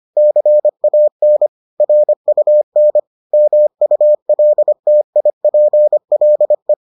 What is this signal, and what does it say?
Morse code, heard as a keyed tone, about 25 wpm, 600 Hz. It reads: CAN RUN MULTIPLE